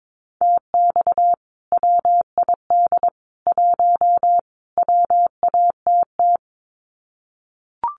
Morse code, heard as a keyed tone, about 22 wpm, 700 Hz. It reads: TX WID 1 WATT
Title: Common QSO Phrases 22wpm